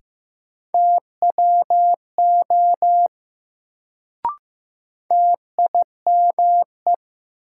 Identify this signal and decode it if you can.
Morse code, heard as a keyed tone, about 15 wpm, 700 Hz. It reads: TWO TIME
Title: Sets of 2 Words - New Word Drill - 15wpm